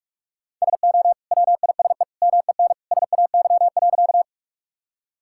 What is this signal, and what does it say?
Morse code, heard as a keyed tone, about 46 wpm, 700 Hz. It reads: SO WISE MEN SAY.
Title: Sentences from Top 1600 Words 45wpm